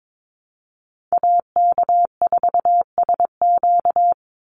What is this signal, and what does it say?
Morse code, heard as a keyed tone, about 22 wpm, 700 Hz. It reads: AK4SQ
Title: Call Signs - 22wpm